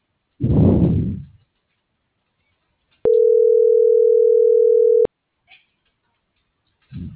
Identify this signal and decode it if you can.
Ringback tone (North America), 440 + 480 Hz.